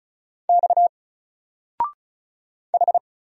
Morse code, heard as a keyed tone, about 35 wpm, 700 Hz. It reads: X H
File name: Single-Letter-Number-35wpm